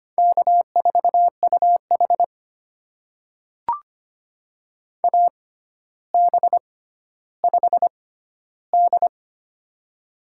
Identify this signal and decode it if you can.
Morse code, heard as a keyed tone, about 25 wpm, spaced as for 13 wpm, 700 Hz. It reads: K4UH AB5D
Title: USA Call Signs - 4 Characters - 6x Character Spacing - 25wpm